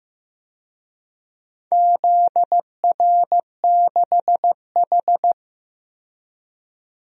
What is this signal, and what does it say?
Morse code, heard as a keyed tone, about 15 wpm, 700 Hz. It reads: ZR6H
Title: DXCC-100 Call Signs - 4 Characters - 15wpm